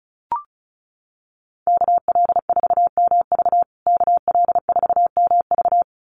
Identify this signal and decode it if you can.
Morse code, heard as a keyed tone, about 34 wpm, 700 Hz. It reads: KL4MV KL4MV
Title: USA Call Signs Repeated 2x 35wpm